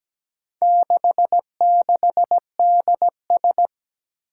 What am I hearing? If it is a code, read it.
Morse code, heard as a keyed tone, about 17 wpm, 700 Hz. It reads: 66DS
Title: Four Challenging Letter-Number Combinations 17wpm